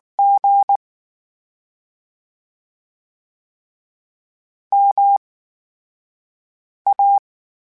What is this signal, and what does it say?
Morse code, heard as a keyed tone, about 19 wpm, spaced as for 5 wpm, 800 Hz. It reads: G MA
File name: Special INGs / Special INGs-mp3s / VerySlowToFaster Ing @18